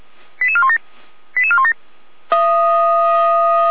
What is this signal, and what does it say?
Five-tone selective call (ZVEI), heard as quick runs of five tones, tones of about 70 ms. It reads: 88417 88417
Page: Server at audio.f85360.feuerwehr-orlamuende.org